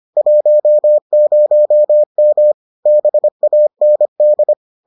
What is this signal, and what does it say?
Morse code, heard as a keyed tone, about 25 wpm, 600 Hz. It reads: 10M BAND